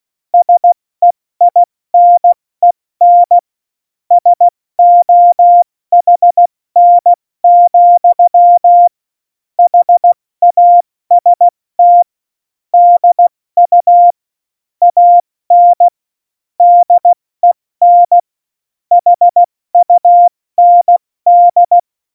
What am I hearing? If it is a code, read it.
Morse code, heard as a keyed tone, about 16 wpm, spaced as for 14 wpm, 700 Hz. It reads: SEINEN SOHN, HAST DU AN DEN HUND
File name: Morsekurs / Witze / Witz 19. September 17, Tempo 70